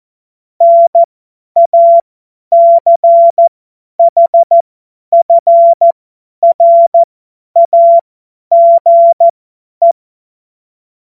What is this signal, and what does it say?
Morse code, heard as a keyed tone, about 14 wpm, spaced as for 10 wpm, 680 Hz. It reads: NACHFRAGE
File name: Morsekurs / Witze / Witz 24.Oktober 2023 50 F70